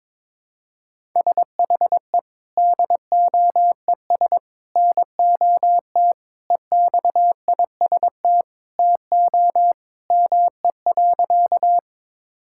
Morse code, heard as a keyed tone, about 22 wpm, 700 Hz. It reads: SHE DOES NOT EXIST TO ME.
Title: Sentences from Top 1000 Words - Review Part 5 - 22wpm